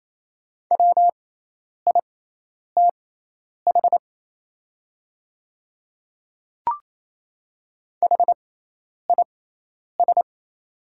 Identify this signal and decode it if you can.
Morse code, heard as a keyed tone, about 28 wpm, spaced as for 10 wpm, 700 Hz. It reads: WITH HIS
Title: 6x Character Spacing - Top 100 Words - 28wpm